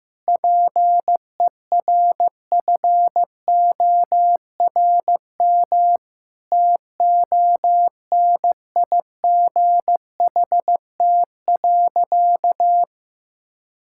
Morse code, heard as a keyed tone, about 15 wpm, 700 Hz. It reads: PERFORM TONIGHT.
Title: Sentences from Top 1000 Words - Review Part 5 - 15wpm